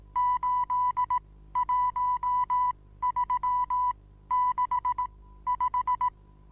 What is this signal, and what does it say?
Morse code, heard as a keyed tone, about 18 wpm, spaced as for 13 wpm, 990 Hz. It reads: 81365
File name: Onlinelehrgang / morsen / DJ4UF Morsen25